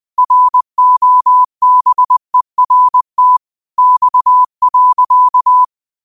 Morse code, heard as a keyed tone, about 20 wpm, 1000 Hz. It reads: ROBERT X.
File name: qotdfiles / 20wpm